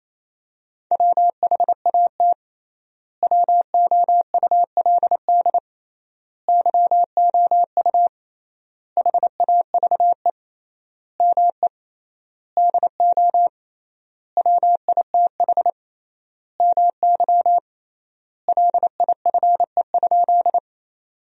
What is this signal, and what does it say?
Morse code, heard as a keyed tone, about 28 wpm, 700 Hz. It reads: WHAT WOULD YOU HAVE ME DO WITH MY LIFE?
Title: 3x Word-Spacing - Sentences from Top 300 Words - 28wpm